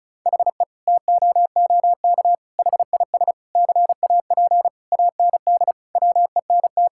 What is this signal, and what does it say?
Morse code, heard as a keyed tone, about 34 wpm, 700 Hz. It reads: HE TOOK HIS CAP AND WENT
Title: Sentences from Top 1600 Words 35wpm